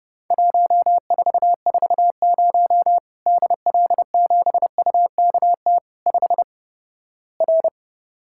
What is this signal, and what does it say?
Morse code, heard as a keyed tone, about 32 wpm, 700 Hz. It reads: 1440 DL7UKT 5 R
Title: Worked All Europe DX Contest - QTC Exchange - 30wpm